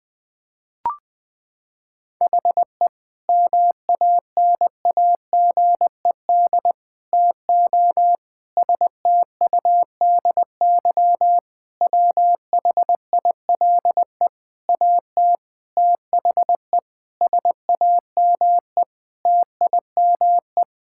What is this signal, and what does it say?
Morse code, heard as a keyed tone, about 20 wpm, 700 Hz. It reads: HE MANAGED TO STUDY WHILE AT THE SAME TIME
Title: Sentences with 5-Word Phrases - 20wpm